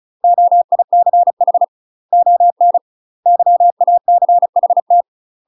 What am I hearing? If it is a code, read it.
Morse code, heard as a keyed tone, about 34 wpm, 700 Hz. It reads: OICH ON YACHT
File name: UK MASTER QSOs TRAINING / 35WPM / UK MASTER QSOs 0037